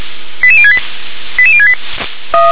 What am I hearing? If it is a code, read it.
Five-tone selective call (ZVEI), heard as quick runs of five tones, tones of about 70 ms. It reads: 88068 88068